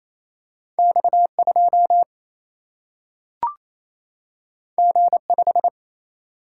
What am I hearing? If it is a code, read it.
Morse code, heard as a keyed tone, about 28 wpm, 700 Hz. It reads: X 2 G 5
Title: Single-Letter Single-Number 28wpm